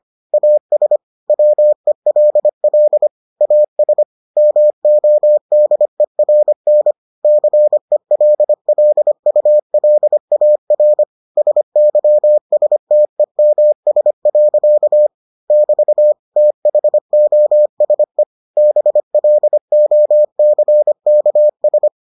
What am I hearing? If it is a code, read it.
Morse code, heard as a keyed tone, about 25 wpm, 600 Hz. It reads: AS WELL AS MODERN CELLULAR SYSTEMS. <BT> THOSE BLOCKS